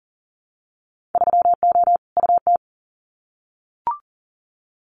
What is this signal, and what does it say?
Morse code, heard as a keyed tone, about 40 wpm, 700 Hz. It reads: 3O UT
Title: Winter Field Day Exchanges 40wpm